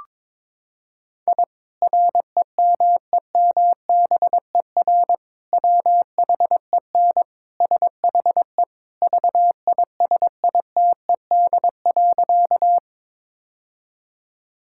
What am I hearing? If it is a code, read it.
Morse code, heard as a keyed tone, about 22 wpm, 700 Hz. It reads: I REMEMBER WHEN SHE VISITED.
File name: practice / 22wpm / Sentences with 2-Word Phrases - 22wpm